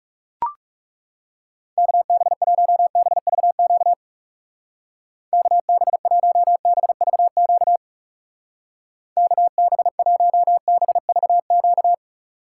Morse code, heard as a keyed tone, about 36 wpm, 700 Hz. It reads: KB1BVQ KB1BVQ KB1BVQ
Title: Speed Racing - USA 2x3 Call Signs 30wpm